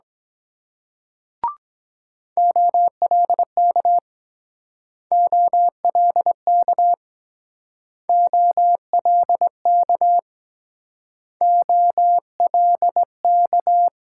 Morse code, heard as a keyed tone, about 22 wpm, 700 Hz. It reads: OLK OLK OLK OLK